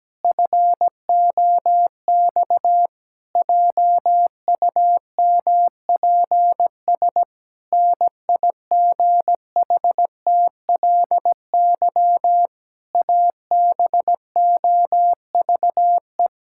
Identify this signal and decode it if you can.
Morse code, heard as a keyed tone, about 17 wpm, 700 Hz. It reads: FOX JUMPS NIGHTLY ABOVE
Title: Pangram Challenge v2 - 17wpm